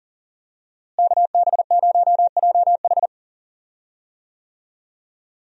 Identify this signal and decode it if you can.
Morse code, heard as a keyed tone, about 39 wpm, 700 Hz. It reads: KB0JH
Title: USA Call Signs - 5 Characters - Repeated 3x - 40wpm